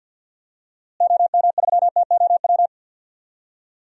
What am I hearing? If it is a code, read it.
Morse code, heard as a keyed tone, about 46 wpm, 700 Hz. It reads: OM3TOW
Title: DXCC-100 Call Signs - 6 Characters - Repeated 2x - 50wpm